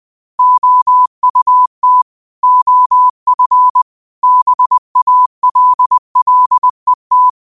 Morse code, heard as a keyed tone, about 20 wpm, 1000 Hz. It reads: OUT OF BALLET
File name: qotdfiles / 20wpm